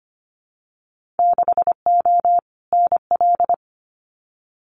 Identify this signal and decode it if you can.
Morse code, heard as a keyed tone, about 25 wpm, 700 Hz. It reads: 6O NL
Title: Winter Field Day Exchanges Encore 25wpm